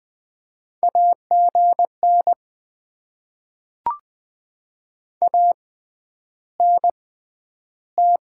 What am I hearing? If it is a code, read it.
Morse code, heard as a keyed tone, about 20 wpm, spaced as for 10 wpm, 700 Hz. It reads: AGN ANT
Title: Top 20 QSO Elements - 6x Character Spacing - 20wpm